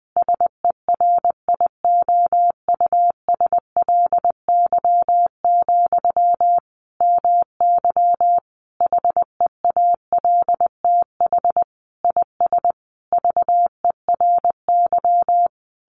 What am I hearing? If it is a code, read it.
Morse code, heard as a keyed tone, about 20 wpm, 700 Hz. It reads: SERIOUSLY, MY HEALTH IS VERY
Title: Sentences from Top 1200 Words 20wpm